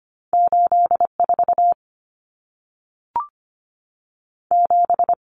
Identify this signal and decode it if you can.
Morse code, heard as a keyed tone, about 25 wpm, 700 Hz. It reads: 84 7
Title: Numbers - 2-Digit 25wpm